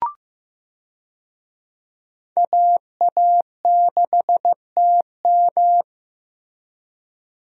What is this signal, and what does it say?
Morse code, heard as a keyed tone, about 15 wpm, 700 Hz. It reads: AA6TM
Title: USA Call Signs - 5 Characters - Repeated 3x - 15wpm